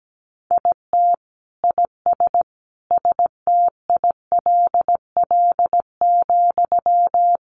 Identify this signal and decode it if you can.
Morse code, heard as a keyed tone, about 17 wpm, 700 Hz. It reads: IT IS STILL,
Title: Sentences from Top 1000 Words 17wpm